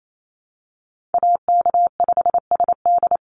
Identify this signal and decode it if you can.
Morse code, heard as a keyed tone, about 28 wpm, 700 Hz. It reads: AK5SD
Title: NA SOTA Chaser Callsigns 28wpm